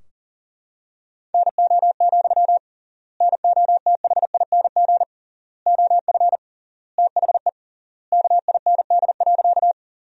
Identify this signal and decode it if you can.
Morse code, heard as a keyed tone, about 40 wpm, 700 Hz. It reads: NO, NOTHING OF THE KIND.
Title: 3x Word-Spacing - Sentences from Top 500 Words - 40wpm